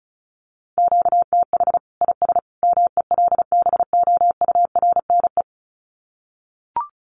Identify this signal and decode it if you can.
Morse code, heard as a keyed tone, about 34 wpm, 700 Hz. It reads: QTH IS MELBOURNE
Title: Common QSO Phrases 35wpm